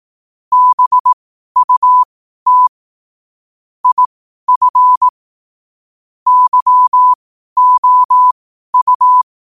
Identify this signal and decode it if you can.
Morse code, heard as a keyed tone, about 18 wpm, spaced as for 12 wpm, 1000 Hz. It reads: BUT IF YOU
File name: qotdfiles / 12wpm